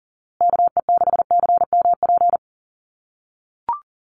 Morse code, heard as a keyed tone, about 40 wpm, 700 Hz. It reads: KE6CMP
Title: USA Call Signs Repeated 3x 40wpm